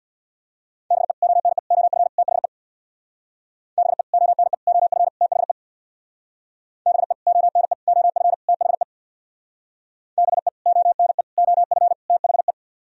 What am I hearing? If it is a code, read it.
Morse code, heard as a keyed tone, about 64 wpm, 700 Hz. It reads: BE ONE OF THE BE ONE OF THE BE ONE OF THE BE ONE OF THE